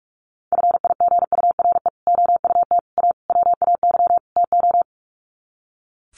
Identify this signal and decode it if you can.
Morse code, heard as a keyed tone, about 44 wpm, 700 Hz. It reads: FIGURE OUT A WAY TO
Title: Sets of 5 Words Encore2 45wpm